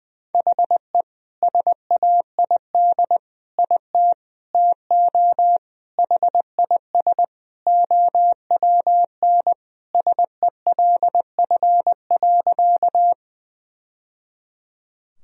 Morse code, heard as a keyed tone, about 19 wpm, 700 Hz. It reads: HE SAID IT TO HIS OWN SELF.
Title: Sentences from Top 1000 Words - Review Part 1 - 20wpm